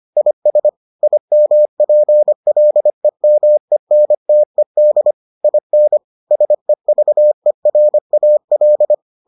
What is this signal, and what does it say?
Morse code, heard as a keyed tone, about 25 wpm, 600 Hz. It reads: IS IMPLEMENTED IN SEVERAL